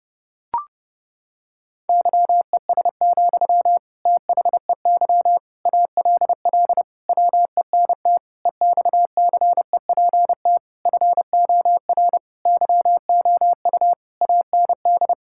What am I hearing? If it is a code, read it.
Morse code, heard as a keyed tone, about 28 wpm, 700 Hz. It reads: YES, THEY ALL WENT EXCEPT FOR YOU AND
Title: Sentences from Top 1600 Words 30wpm